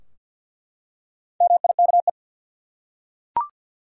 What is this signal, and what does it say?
Morse code, heard as a keyed tone, about 51 wpm, 700 Hz. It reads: MIKE
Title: USA Common Names - Max 4 Letters - 50wpm